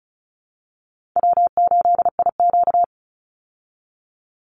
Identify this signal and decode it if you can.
Morse code, heard as a keyed tone, about 34 wpm, 700 Hz. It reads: W8IQ
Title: Easy Call Signs Repeated 3x - 35wpm